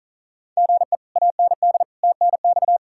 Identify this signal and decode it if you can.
Morse code, heard as a keyed tone, about 47 wpm, 700 Hz. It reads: GE AND TNX